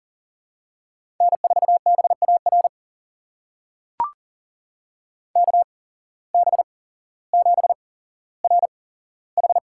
Morse code, heard as a keyed tone, about 40 wpm, spaced as for 17 wpm, 700 Hz. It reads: N4BAR KB7RH
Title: USA Call Signs - 8x Character Spacing - 40wpm